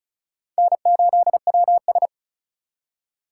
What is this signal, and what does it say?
Morse code, heard as a keyed tone, about 47 wpm, spaced as for 41 wpm, 700 Hz. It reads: N8WS
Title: Speed Racing - USA 1x2 Call Signs 35wpm